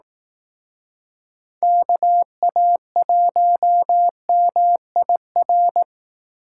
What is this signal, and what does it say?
Morse code, heard as a keyed tone, about 18 wpm, 700 Hz. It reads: KA1MIR
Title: Speed Racing - USA 2x3 Call Signs 15wpm